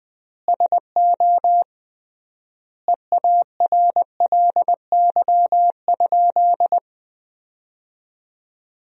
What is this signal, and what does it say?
Morse code, heard as a keyed tone, about 20 wpm, 700 Hz. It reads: SO EARLY?
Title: 3x Word-Spacing - Sentences from Top 400 Words - 20wpm